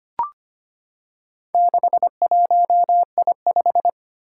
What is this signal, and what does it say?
Morse code, heard as a keyed tone, about 25 wpm, 700 Hz. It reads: 61I5
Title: Four Challenging Letter-Number Combinations 25wpm